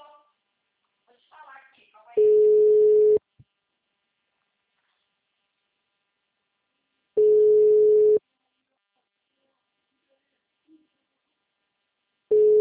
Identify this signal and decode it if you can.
Ringback tone (Europe, 425 Hz), with speech.